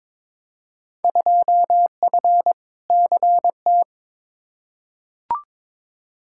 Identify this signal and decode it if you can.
Morse code, heard as a keyed tone, about 22 wpm, 700 Hz. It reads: 2F CT
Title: Field Day Exchanges Encore - 22wpm